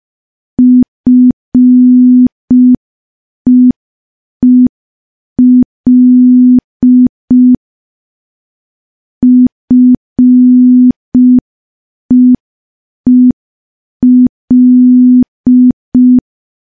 Morse code, heard as a keyed tone, about 5 wpm, 250 Hz. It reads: FEEL FEEL
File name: morse / morse-training-05WPM